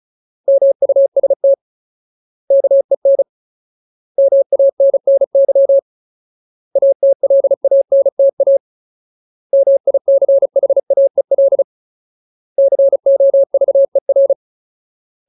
Morse code, heard as a keyed tone, about 34 wpm, 550 Hz. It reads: MUST KEN MANNY ATLANTA MICHAEL COVER
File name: Ultimate Word List / 35WPM Ultimate